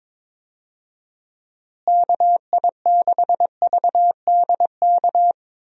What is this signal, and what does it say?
Morse code, heard as a keyed tone, about 22 wpm, 700 Hz. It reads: KI6VDK
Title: USA Call Signs - 6 Characters - Repeated 2x - 22wpm